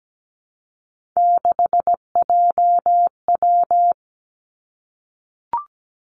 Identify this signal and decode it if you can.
Morse code, heard as a keyed tone, about 17 wpm, 700 Hz. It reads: 6JW
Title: Three Challenging Letter-Number Combinations 17wpm